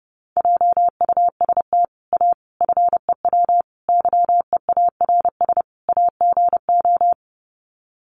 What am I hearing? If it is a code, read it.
Morse code, heard as a keyed tone, about 30 wpm, 700 Hz. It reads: JUST A FEW YEARS AGO
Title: Sets of 5 Words Encore 30wpm